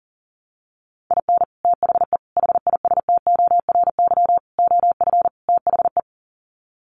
Morse code, heard as a keyed tone, about 40 wpm, 700 Hz. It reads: IN THE HISTORY OF THE
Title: Sets of 5 Words 40wpm